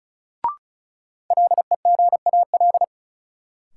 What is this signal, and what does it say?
Morse code, heard as a keyed tone, about 36 wpm, 700 Hz. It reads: LEGAL
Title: Sentences from Top 1200 Words - New Word Drill - 35wpm